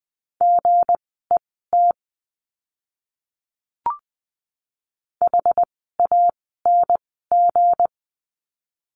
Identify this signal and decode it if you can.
Morse code, heard as a keyed tone, about 20 wpm, spaced as for 15 wpm, 700 Hz. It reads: GET HANG
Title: 160 Lesson 26f - Letter g - Words 20wpm